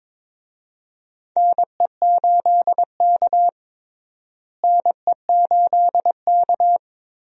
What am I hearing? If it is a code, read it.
Morse code, heard as a keyed tone, about 22 wpm, 700 Hz. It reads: NE8K NE8K
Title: USA Call Signs - 4 Characters - Repeated 3x - 22wpm